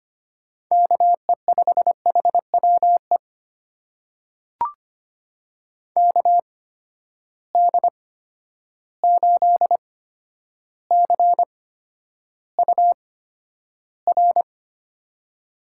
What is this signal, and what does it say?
Morse code, heard as a keyed tone, about 25 wpm, spaced as for 11 wpm, 700 Hz. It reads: KE5HWE KD8CUR
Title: USA Call Signs - 6 Characters - 8x Character Spacing - 25wpm